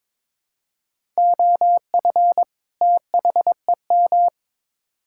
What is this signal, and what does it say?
Morse code, heard as a keyed tone, about 22 wpm, 700 Hz. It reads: OF THEM
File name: practice / 22wpm / Build A Phrase - 2-Word Phrases - 22wpm